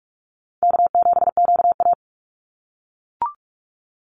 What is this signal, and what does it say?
Morse code, heard as a keyed tone, about 44 wpm, 700 Hz. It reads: K7QA
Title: Call Signs - 45wpm